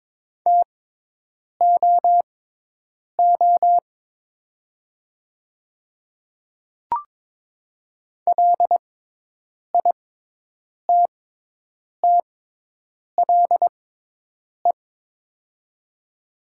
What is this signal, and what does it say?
Morse code, heard as a keyed tone, about 22 wpm, spaced as for 8 wpm, 700 Hz. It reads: TOO LITTLE
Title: 6x Character Spacing - Top 200 Words - 22wpm